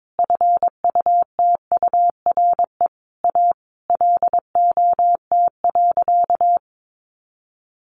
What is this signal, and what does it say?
Morse code, heard as a keyed tone, about 22 wpm, 700 Hz. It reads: FUTURE A LOT.
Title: Easy Phrases 22wpm